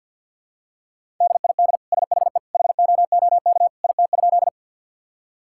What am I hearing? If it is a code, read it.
Morse code, heard as a keyed tone, about 50 wpm, 700 Hz. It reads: DID SHE HOOK IT?
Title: Sentences from Top 1600 Words 50wpm